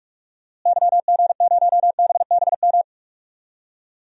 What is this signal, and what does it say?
Morse code, heard as a keyed tone, about 46 wpm, 700 Hz. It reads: YG0BBM
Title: DXCC-100 Call Signs - 6 Characters - Repeated 3x - 45wpm